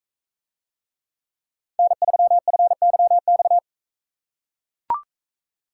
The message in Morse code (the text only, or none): N3FYX